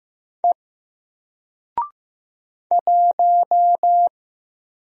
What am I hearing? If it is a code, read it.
Morse code, heard as a keyed tone, about 15 wpm, 700 Hz. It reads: E 1
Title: Single Character (Letter:Number:Punctuation:Prosign) Alt 15wpm